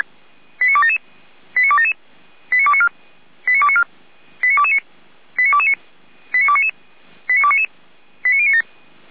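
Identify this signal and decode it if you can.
Five-tone selective call (ZVEI), heard as quick runs of five tones, tones of about 70 ms. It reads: 89277 89277 89294 89294 89229 89229 89299 89299 89097
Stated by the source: ZVEI five-tone call 89310 received by SDR.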